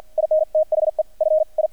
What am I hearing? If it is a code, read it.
Morse code, heard as a keyed tone, about 48 wpm, 620 Hz. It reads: IN THE FI
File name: archived / CRSnet